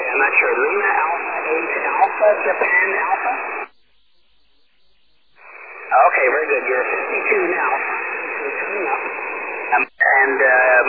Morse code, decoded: none